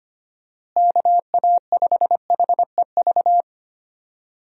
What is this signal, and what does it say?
Morse code, heard as a keyed tone, about 25 wpm, 700 Hz. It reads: KA5HEV
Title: USA Call Signs - 6 Characters - Repeated 3x - 25wpm